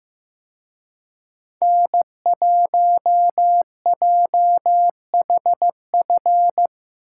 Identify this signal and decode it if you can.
Morse code, heard as a keyed tone, about 18 wpm, 700 Hz. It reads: N1JHF
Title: Speed Racing - USA Call Signs 15wpm